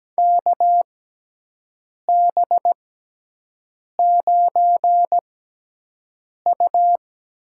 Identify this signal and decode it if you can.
Morse code, heard as a keyed tone, about 17 wpm, spaced as for 6 wpm, 700 Hz. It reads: KB9U
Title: USA Call Signs - 6 Characters - 6x Character Spacing - 17wpm